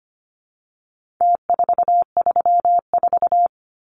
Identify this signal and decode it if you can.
Morse code, heard as a keyed tone, about 25 wpm, 700 Hz. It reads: T434
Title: Numbers - Contest Style - 4-Digit 25wpm